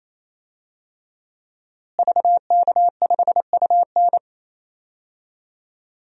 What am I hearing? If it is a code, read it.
Morse code, heard as a keyed tone, about 28 wpm, 700 Hz. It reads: VK5UN
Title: DXCC-100 Call Signs - 5 Characters - Repeated 3x - 28wpm